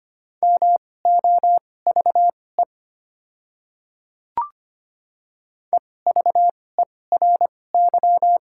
Morse code, heard as a keyed tone, about 25 wpm, spaced as for 18 wpm, 700 Hz. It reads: MOVE EVERY
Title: 2x Character Spacing - Top 500 Words - 25wpm